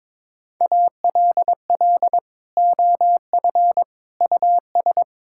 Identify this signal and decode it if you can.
Morse code, heard as a keyed tone, about 21 wpm, 700 Hz. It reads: ALL OF US
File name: practice / 22wpm / Sentences with 2-Word Phrases - 22wpm